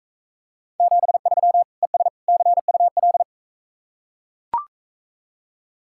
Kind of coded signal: Morse code

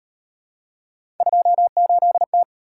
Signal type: Morse code

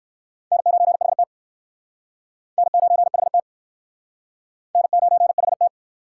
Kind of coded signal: Morse code

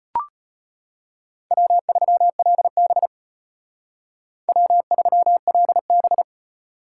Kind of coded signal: Morse code

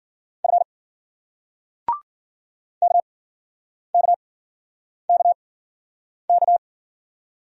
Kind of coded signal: Morse code

